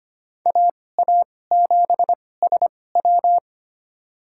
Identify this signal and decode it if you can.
Morse code, heard as a keyed tone, about 25 wpm, spaced as for 18 wpm, 700 Hz. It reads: AA7SW
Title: USA Call Signs - 5 Characters - 2x Character Spacing - 25wpm